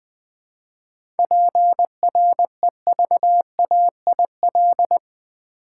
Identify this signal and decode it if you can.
Morse code, heard as a keyed tone, about 20 wpm, 700 Hz. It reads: PREVAIL
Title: 7 Letter Words - First 3 Letters Given - 20wpm